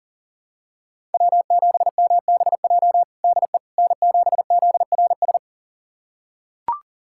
Morse code, heard as a keyed tone, about 37 wpm, 700 Hz. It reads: W7MBJ DE N7ZRS
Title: USA Callsign DE Callsign - 5 Characters - 40wpm